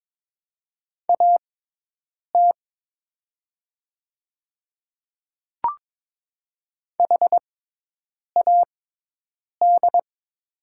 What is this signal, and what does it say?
Morse code, heard as a keyed tone, about 22 wpm, spaced as for 8 wpm, 700 Hz. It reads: AT HAD
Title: 6x Character Spacing - Top 300 Words - 22wpm